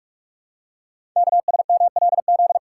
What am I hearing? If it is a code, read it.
Morse code, heard as a keyed tone, about 44 wpm, 700 Hz. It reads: KSMLZ
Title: Code Groups - 5 Letters - 45wpm